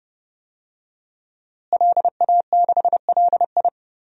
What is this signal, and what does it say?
Morse code, heard as a keyed tone, about 30 wpm, 700 Hz. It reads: LA6LI
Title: DXCC-100 Call Signs - 5 Characters - Repeated 2x - 30wpm